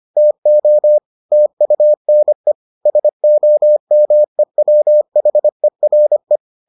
Morse code, heard as a keyed tone, about 25 wpm, 600 Hz. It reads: TO TUNE SOMEWHERE